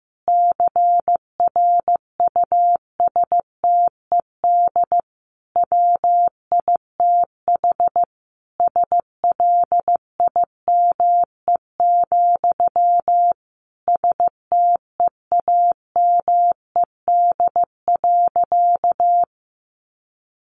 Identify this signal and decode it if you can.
Morse code, heard as a keyed tone, about 15 wpm, 700 Hz. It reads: CRUSTED WITH SLIME, STEAMED.
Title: A Sound Of Thunder 15wpm